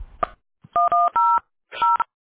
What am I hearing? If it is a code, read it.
DTMF, heard as telephone touch tones, tones of about 240 ms. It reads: 100